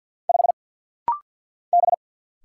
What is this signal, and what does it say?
Morse code, heard as a keyed tone, about 49 wpm, 700 Hz. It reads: 5 B
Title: Single Letter-Number Alt - Rapid-Fire 50wpm